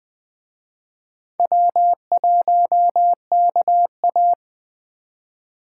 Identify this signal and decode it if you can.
Morse code, heard as a keyed tone, about 20 wpm, 700 Hz. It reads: W1KA